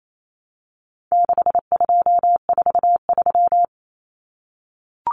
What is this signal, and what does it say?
Morse code, heard as a keyed tone, about 28 wpm, 700 Hz. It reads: 6243
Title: Numbers - 4-Digit 28wpm